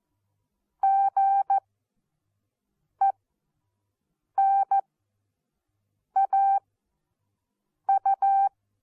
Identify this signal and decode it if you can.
Morse code, heard as a keyed tone, about 14 wpm, spaced as for 5 wpm, 780 Hz. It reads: GENAU
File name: Onlinelehrgang / morsen / DJ4UF Morsen14